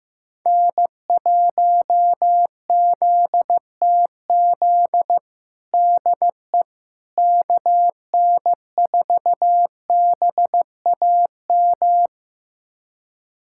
Morse code, heard as a keyed tone, about 15 wpm, 700 Hz. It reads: N1ZTZ DE KN4BAM
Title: USA Callsign DE Callsign 15wpm